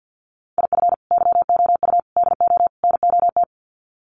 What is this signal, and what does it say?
Morse code, heard as a keyed tone, about 50 wpm, 700 Hz. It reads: IF YOU DO NOT
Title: Sets of 4 Words 50wpm